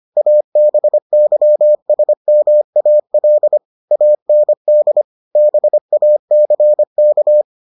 Morse code, heard as a keyed tone, about 25 wpm, 600 Hz. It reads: ABYSMAL AND BACK